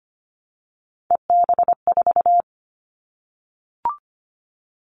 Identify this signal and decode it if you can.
Morse code, heard as a keyed tone, about 25 wpm, 700 Hz. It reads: EB4
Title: Callsign Prefixes for Top 25 DXCC - 25wpm